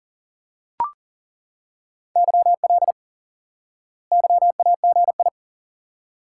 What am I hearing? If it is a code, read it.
Morse code, heard as a keyed tone, about 37 wpm, 700 Hz. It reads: YL YAGI
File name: IQE Course 3 - Lesson 89 - YAGI - Two Elements - 40wpm